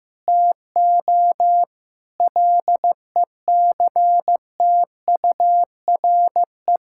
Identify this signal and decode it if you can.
Morse code, heard as a keyed tone, about 15 wpm, 700 Hz. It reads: TO LECTURE
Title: Sentences from Top 1600 Words 15wpm